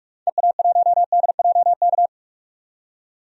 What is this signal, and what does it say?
Morse code, heard as a keyed tone, about 42 wpm, 700 Hz. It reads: EA1DJK
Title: DXCC-100 Call Signs - 6 Characters - 45wpm